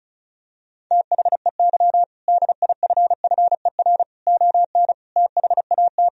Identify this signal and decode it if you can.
Morse code, heard as a keyed tone, about 34 wpm, 700 Hz. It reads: THEY DIFFER ON THAT
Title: Sentences from Top 1000 Words - Review Part 1 - 35wpm